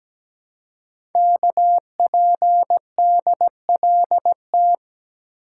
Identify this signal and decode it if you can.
Morse code, heard as a keyed tone, about 17 wpm, 700 Hz. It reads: KPDLT
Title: Code Groups - 5 Letters - 17wpm